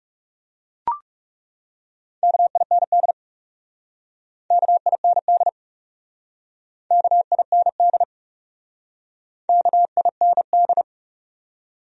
Morse code, heard as a keyed tone, about 40 wpm, 700 Hz. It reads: KIND KIND KIND KIND